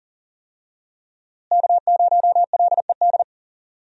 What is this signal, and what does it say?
Morse code, heard as a keyed tone, about 40 wpm, 700 Hz. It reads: K0LED
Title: USA Call Signs - 5 Characters - Repeated 2x - 40wpm